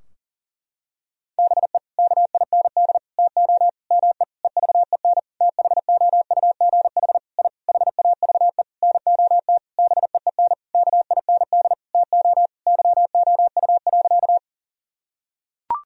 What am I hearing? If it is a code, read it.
Morse code, heard as a keyed tone, about 40 wpm, 700 Hz. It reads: BE KIND TO ME EVEN THOUGH I HAVE NOT BEEN KIND TO YOU.
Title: Sentences from Top 1000 Words - Review Part 2 - 40wpm